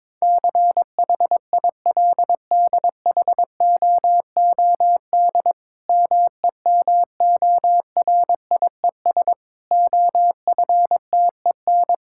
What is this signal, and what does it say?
Morse code, heard as a keyed tone, about 22 wpm, 700 Hz. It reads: CHILDHOOD MEMORIES OFTEN
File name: Difficult Sentences with Hints - 22wpm